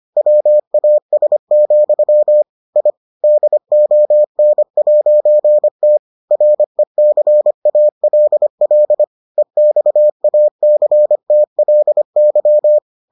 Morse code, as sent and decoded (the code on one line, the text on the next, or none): .-- .- ... --..-- / .. / -.. --- -. .----. - / .-. -.-. .- .-.. .-.. / . -..- .- -.-. - .-.. -.--
WAS, I DON'T RECALL EXACTLY